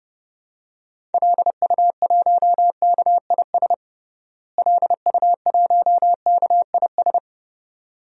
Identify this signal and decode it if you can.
Morse code, heard as a keyed tone, about 32 wpm, 700 Hz. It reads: LU1KIS LU1KIS
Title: DXCC-100 Call Signs - 6 Characters - Repeated 2x - 30wpm